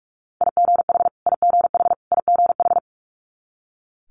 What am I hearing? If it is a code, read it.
Morse code, heard as a keyed tone, about 46 wpm, 700 Hz. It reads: IGH IGH IGH
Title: Multi-letter Phonemes Repeated 3x 45wpm